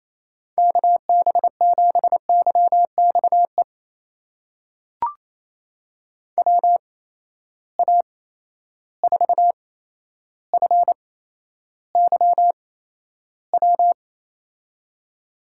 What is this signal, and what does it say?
Morse code, heard as a keyed tone, about 28 wpm, spaced as for 12 wpm, 700 Hz. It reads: KB7YXE WA4FYW
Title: USA Call Signs - 6 Characters - 8x Character Spacing - 28wpm